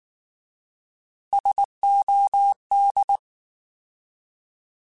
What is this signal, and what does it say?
Morse code, heard as a keyed tone, about 19 wpm, 800 Hz. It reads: SOD
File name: Alphabetical / Alphabetical mp3s / Ss @18